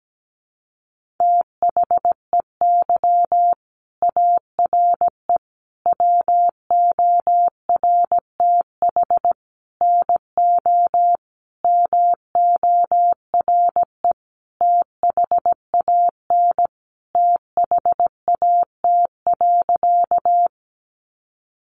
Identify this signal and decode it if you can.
Morse code, heard as a keyed tone, about 17 wpm, 700 Hz. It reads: THEY ARE WORTH NO MORE THAN THAT.
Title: Sentences from Top 800 Words 17wpm